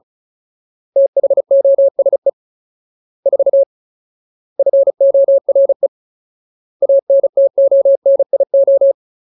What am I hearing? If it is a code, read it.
Morse code, heard as a keyed tone, about 35 wpm, 550 Hz. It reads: THOSE 4 FORE ANTONIO
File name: Ultimate Word List / 35WPM Ultimate